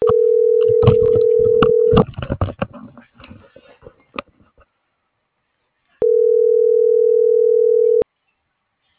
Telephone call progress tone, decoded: ringback tone

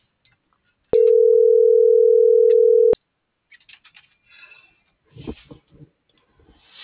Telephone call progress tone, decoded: ringback tone

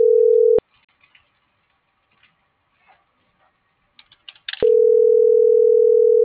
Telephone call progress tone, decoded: ringback tone

